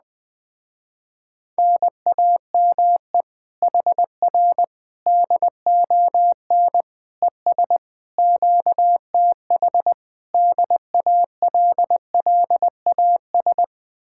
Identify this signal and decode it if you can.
Morse code, heard as a keyed tone, about 20 wpm, 700 Hz. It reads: NAME HR DON ES QTH DALLAS
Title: Speed-Racing Common QSO Phrases - 17wpm